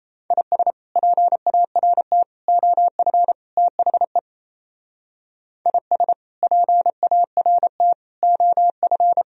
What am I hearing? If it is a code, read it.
Morse code, heard as a keyed tone, about 37 wpm, 700 Hz. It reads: IS PART OF THE IS PART OF